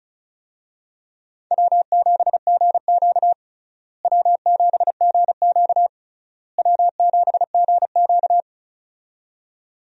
Morse code, heard as a keyed tone, about 34 wpm, 700 Hz. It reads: W7GQ W7GQ W7GQ
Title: USA Call Signs - 4 Characters - Repeated 3x - 35wpm